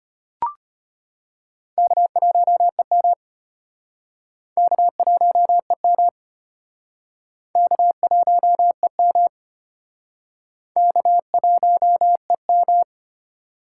Morse code, heard as a keyed tone, about 33 wpm, 700 Hz. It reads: K1EM K1EM K1EM K1EM